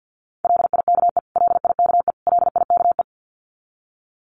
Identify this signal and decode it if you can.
Morse code, heard as a keyed tone, about 50 wpm, 700 Hz. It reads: LIKE LIKE LIKE
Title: Top 300 Words Repeated 50wpm